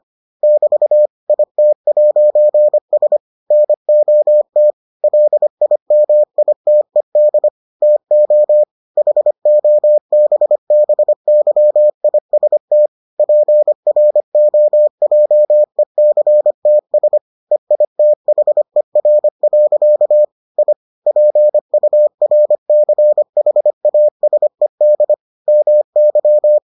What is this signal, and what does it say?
Morse code, heard as a keyed tone, about 25 wpm, 600 Hz. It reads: <BT> IT'S NOT LIMITED TO HOBBYIST PROJECTS EITHER. I PURCHASED MY